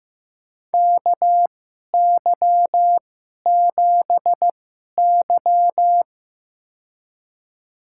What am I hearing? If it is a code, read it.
Morse code, heard as a keyed tone, about 15 wpm, spaced as for 11 wpm, 700 Hz. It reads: KY7Y
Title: USA Call Signs - 4 Characters - 2x Character Spacing - 15wpm